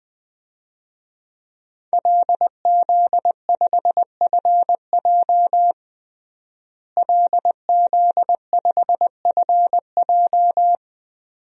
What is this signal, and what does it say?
Morse code, heard as a keyed tone, about 20 wpm, 700 Hz. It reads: LZ5FJ LZ5FJ
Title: DXCC-100 Call Signs - 5 Characters - Repeated 2x - 20wpm